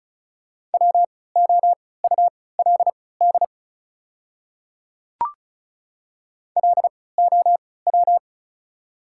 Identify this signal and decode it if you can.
Morse code, heard as a keyed tone, about 34 wpm, spaced as for 20 wpm, 700 Hz. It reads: WOULD LOW
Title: 3x Character Spacing - Top 500 Words - 35wpm